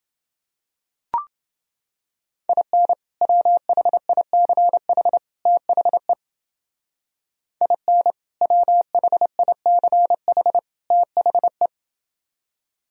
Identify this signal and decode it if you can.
Morse code, heard as a keyed tone, about 31 wpm, 700 Hz. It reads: IN WHICH THE IN WHICH THE